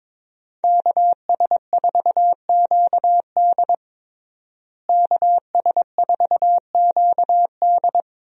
Morse code, heard as a keyed tone, about 22 wpm, 700 Hz. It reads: KS4QD KS4QD
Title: USA Call Signs - 5 Characters - Repeated 2x - 22wpm